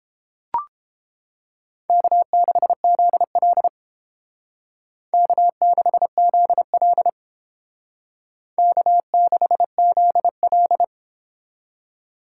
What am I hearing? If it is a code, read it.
Morse code, heard as a keyed tone, about 28 wpm, 700 Hz. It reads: K6ZL K6ZL K6ZL